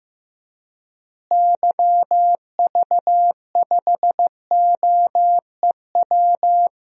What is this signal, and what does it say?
Morse code, heard as a keyed tone, about 15 wpm, 700 Hz. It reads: YV5OEW
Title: DXCC-100 Call Signs - 6 Characters - Repeated 2x - 15wpm